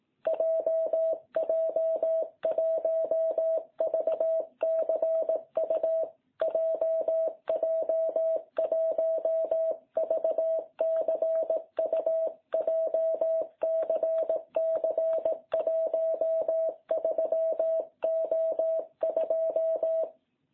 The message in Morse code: JJ1VCU JJ1VCU JCC 13O2

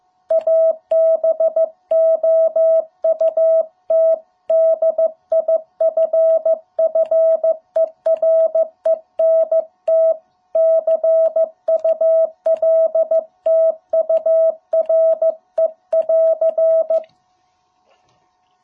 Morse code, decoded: ABOUT DIFFERENT CULTURE <AR>

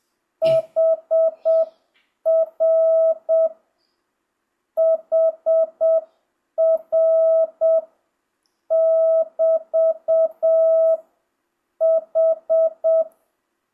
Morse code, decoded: HR HR <BT> H

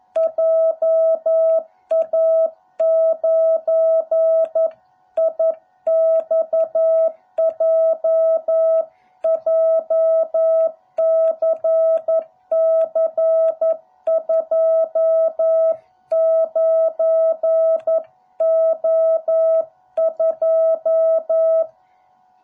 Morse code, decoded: JA9IXJ JCC29O2

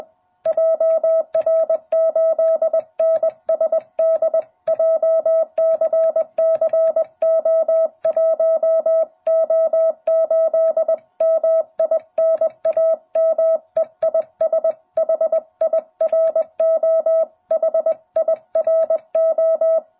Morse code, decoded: JR8NSD JCCO1O8 MI NAME IS HIRO HIRO

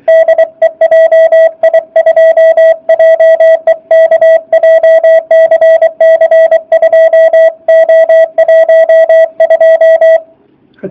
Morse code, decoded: DE JI2JEK JCC 2O12

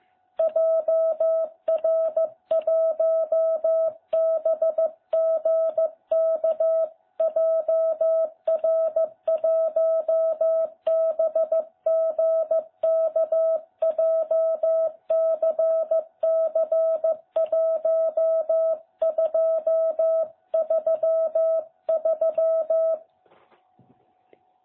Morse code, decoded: JR1BGK JR1BGK JCC1233

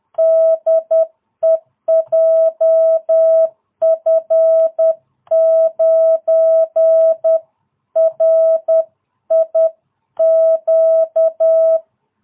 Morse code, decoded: DE JF9RIQ

